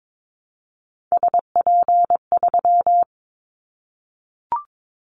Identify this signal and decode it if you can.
Morse code, heard as a keyed tone, about 22 wpm, 700 Hz. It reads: SP3
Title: Callsign Prefixes for Top 25 DXCC - Repeated 3x - 22wpm